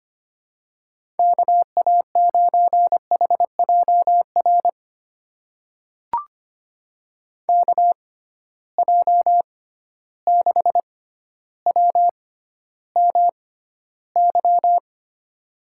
Morse code, heard as a keyed tone, about 25 wpm, spaced as for 13 wpm, 700 Hz. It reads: KA9HJR KJ6WMY
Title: USA Call Signs - 6 Characters - 6x Character Spacing - 25wpm